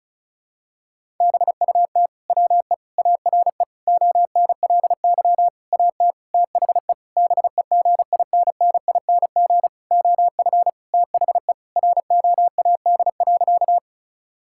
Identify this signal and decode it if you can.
Morse code, heard as a keyed tone, about 36 wpm, 700 Hz. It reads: BUT WE ARE ONLY AT THE BEGINNING OF THE ROAD.
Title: Sentences from Top 1000 Words - Review Part 5 - 35wpm